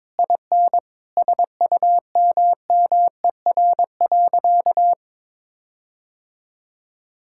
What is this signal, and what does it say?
Morse code, heard as a keyed tone, about 21 wpm, 700 Hz. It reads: IN SUMMER.
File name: practice / 22wpm / Sentences with 8-Word Phrases - 22wpm